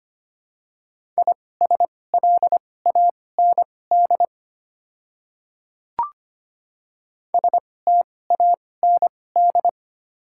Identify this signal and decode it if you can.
Morse code, heard as a keyed tone, about 25 wpm, spaced as for 18 wpm, 690 Hz. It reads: ISLAND STAND
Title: 2x Character Spacing - Top 500 Words - 25wpm